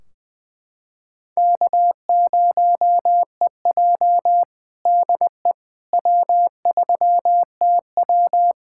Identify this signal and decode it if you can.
Morse code, heard as a keyed tone, about 20 wpm, 700 Hz. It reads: K0EJ DE W3TW
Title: Practice QSO - Exchange 4 - Ending - 20wpm